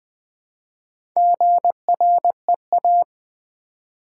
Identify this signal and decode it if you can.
Morse code, heard as a keyed tone, about 20 wpm, 700 Hz. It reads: GREA
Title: Build A Word - 1 Letter At A Time - 5-Letter Words - 20wpm